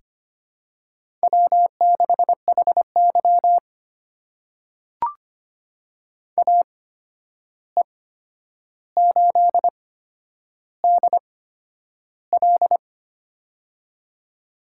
Morse code, heard as a keyed tone, about 25 wpm, spaced as for 10 wpm, 700 Hz. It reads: W6HY AE8DL
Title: USA Call Signs - 8x Character Spacing - 25wpm